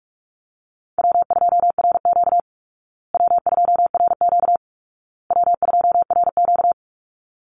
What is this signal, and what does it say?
Morse code, heard as a keyed tone, about 44 wpm, 700 Hz. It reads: W2RQ W2RQ W2RQ
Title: Easy Call Signs Repeated 3x - 45wpm